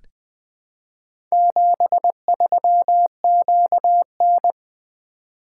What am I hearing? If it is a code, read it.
Morse code, heard as a keyed tone, about 20 wpm, 700 Hz. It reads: 73QN
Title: Four Letter-Number Combinations - 20wpm